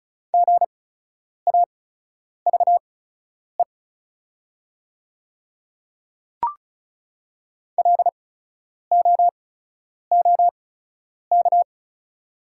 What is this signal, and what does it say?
Morse code, heard as a keyed tone, about 36 wpm, spaced as for 10 wpm, 700 Hz. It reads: GAVE LOOK
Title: 8x Character Spacing - Top 500 Words - 35wpm